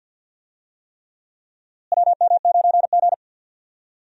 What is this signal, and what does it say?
Morse code, heard as a keyed tone, about 52 wpm, 700 Hz. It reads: WM9G
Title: USA Call Signs - 4 Characters - Repeated 3x - 50wpm